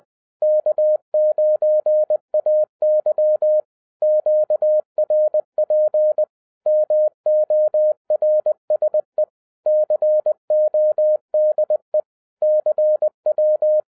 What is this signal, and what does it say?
Morse code, heard as a keyed tone, about 20 wpm, 610 Hz. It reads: K9AY QRP MORSE CODE CW
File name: 2011-07-mp3 / n23 QRP OTAKU 20wpm